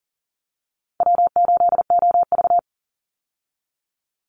Morse code, heard as a keyed tone, about 40 wpm, 700 Hz. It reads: W8OV
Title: Easy Call Signs Repeated 2x - 40wpm